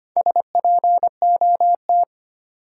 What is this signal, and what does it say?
Morse code, heard as a keyed tone, about 25 wpm, 700 Hz. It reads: SPOT
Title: Sentences from Top 1100 Words - New Word Drill Alt - 25wpm